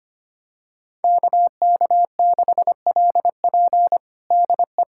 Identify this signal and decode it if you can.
Morse code, heard as a keyed tone, about 25 wpm, 700 Hz. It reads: KK6LP DE
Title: USA Callsign DE Callsign - 5 Characters - 25wpm